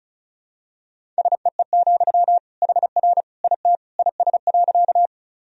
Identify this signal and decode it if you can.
Morse code, heard as a keyed tone, about 36 wpm, 700 Hz. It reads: SEE, HR IT IS.
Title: Sentences from Top 200 Words and Abbreviations 35wpm